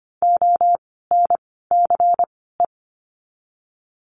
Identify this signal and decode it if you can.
Morse code, heard as a keyed tone, about 25 wpm, spaced as for 16 wpm, 700 Hz. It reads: ONCE
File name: course / 25:10wpm / 094 Lesson 16f - Letter c - Words 25wpm